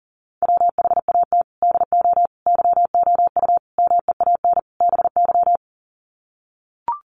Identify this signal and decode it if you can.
Morse code, heard as a keyed tone, about 40 wpm, 700 Hz. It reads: WHAT DO YOU MEAN BY
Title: Sets of 5 Words 40wpm